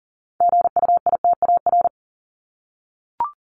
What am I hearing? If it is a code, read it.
Morse code, heard as a keyed tone, about 40 wpm, 700 Hz. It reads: GUITAR
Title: Top 1501-1750 Words 40wpm